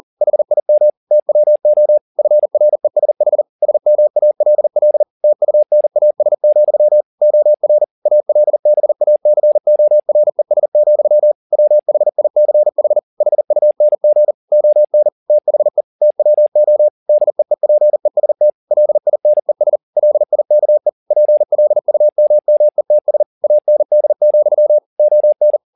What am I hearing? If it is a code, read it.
Morse code, heard as a keyed tone, about 40 wpm, 600 Hz. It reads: HIM TWO FRESH SMALL TUNAS, OR ALBACORES, WHICH HUNG ON THE TWO DEEPEST LINES LIKE PLUMMETS AND, ON